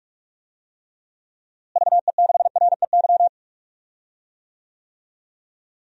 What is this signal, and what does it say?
Morse code, heard as a keyed tone, about 41 wpm, 700 Hz. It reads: VE6REY
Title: DXCC-100 Call Signs - 6 Characters - Repeated 3x - 45wpm